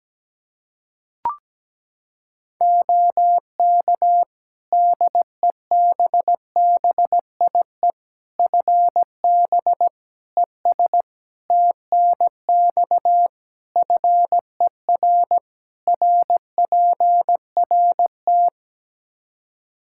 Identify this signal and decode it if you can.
Morse code, heard as a keyed tone, about 17 wpm, 700 Hz. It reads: OK DEBBIE FB ES TNX FER RPRT
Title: Practice QSO - Exchange 2 - Info - 17wpm